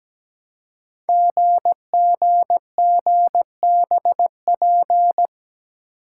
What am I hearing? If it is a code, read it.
Morse code, heard as a keyed tone, about 17 wpm, 700 Hz. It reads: GGGBP
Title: Code Groups - 5 Letters - 17wpm